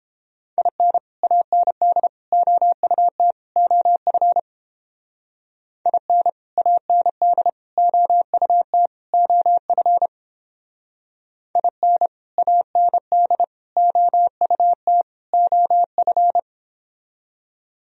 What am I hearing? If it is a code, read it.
Morse code, heard as a keyed tone, about 31 wpm, 700 Hz. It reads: IN AND OUT OF IN AND OUT OF IN AND OUT OF